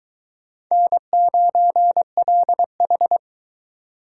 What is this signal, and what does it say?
Morse code, heard as a keyed tone, about 24 wpm, 700 Hz. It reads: N9LH